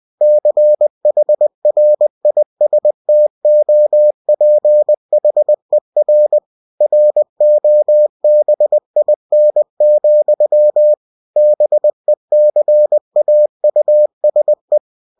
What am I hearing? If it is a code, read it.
Morse code, heard as a keyed tone, about 20 wpm, 600 Hz. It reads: CHRISTOPHER ROBIN, BECAUSE
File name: cw-books / winnie-the-pooh / winnie-20wpm-page-0019